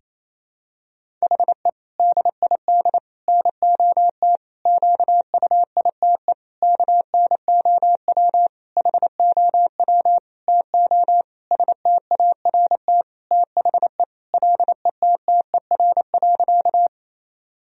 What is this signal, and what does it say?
Morse code, heard as a keyed tone, about 28 wpm, 700 Hz. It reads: HE DID NOT QUITE KNOW HOW TO START THE LETTER.
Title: Sentences from Top 1000 Words - Review Part 1 - 28wpm